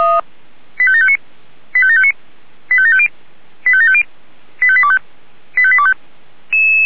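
Five-tone selective call (ZVEI), heard as quick runs of five tones, tones of about 70 ms. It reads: 85759 85759 85750 85750 85825 85825
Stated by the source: ZVEI five-tone selective call received by SDR.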